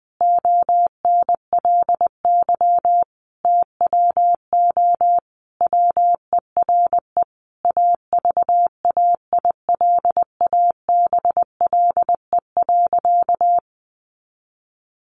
Morse code, heard as a keyed tone, about 20 wpm, 700 Hz. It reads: ONLY TWO WERE AVAILABLE.
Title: Sentences from Top 1100 Words 20wpm